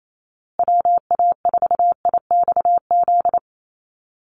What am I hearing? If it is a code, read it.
Morse code, heard as a keyed tone, about 28 wpm, 700 Hz. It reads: WA4IXZ
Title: USA Call Signs Repeated 3x 28wpm